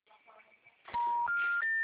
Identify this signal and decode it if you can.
Special information tone.